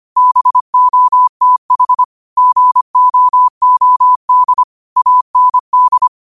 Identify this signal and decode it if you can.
Morse code, heard as a keyed tone, about 25 wpm, 1000 Hz. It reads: BOTH GOOD AND
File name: 25wpm